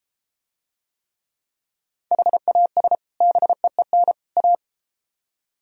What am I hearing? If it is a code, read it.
Morse code, heard as a keyed tone, about 33 wpm, 700 Hz. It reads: HAS BEEN A